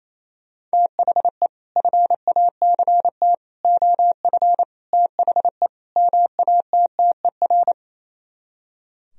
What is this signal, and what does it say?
Morse code, heard as a keyed tone, about 36 wpm, 700 Hz. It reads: THE FACT OF THE MATTER